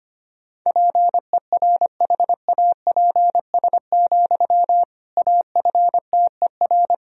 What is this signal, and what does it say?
Morse code, heard as a keyed tone, about 25 wpm, 700 Hz. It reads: PERHAPS, AFTER
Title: Sentences from Top 500 Words and Abbreviations 25wpm